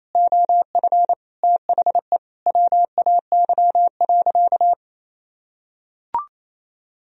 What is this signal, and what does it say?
Morse code, heard as a keyed tone, about 28 wpm, 700 Hz. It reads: OF THE WAY.
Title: Sentences from Top 1000 Words - Review Part 1 - 28wpm